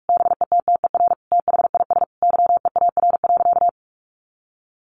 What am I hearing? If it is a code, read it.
Morse code, heard as a keyed tone, about 44 wpm, 700 Hz. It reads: BETTER THIS YEAR.
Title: Sentences from Top 1500 Words 45wpm